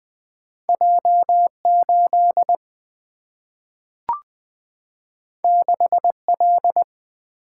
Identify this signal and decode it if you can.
Morse code, heard as a keyed tone, about 20 wpm, 700 Hz. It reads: J 8 6 L
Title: Single-Letter Single-Number 20wpm